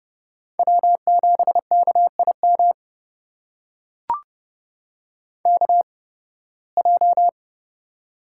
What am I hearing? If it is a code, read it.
Morse code, heard as a keyed tone, about 29 wpm, spaced as for 19 wpm, 700 Hz. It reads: W7KIM KJ
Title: USA Call Signs - 5 Characters - 8x Character Spacing - 30wpm